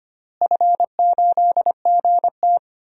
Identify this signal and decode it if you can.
Morse code, heard as a keyed tone, about 25 wpm, 700 Hz. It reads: F8GT
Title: DXCC-100 Call Signs - 4 Characters - Repeated 3x - 25wpm